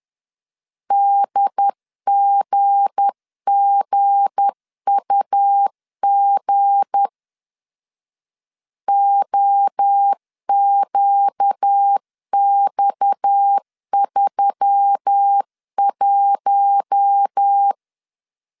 Morse code, heard as a keyed tone, about 10 wpm, 800 Hz. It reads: DGGUG OQX31
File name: LEZIONI / 7 GF9Z / 7 G 10wpm 2501